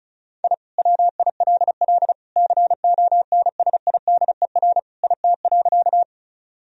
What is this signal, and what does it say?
Morse code, heard as a keyed tone, about 34 wpm, 700 Hz. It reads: I WILL CONSIDER IT.
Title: Sentences from Top 1000 Words - Review Part 4 - 35wpm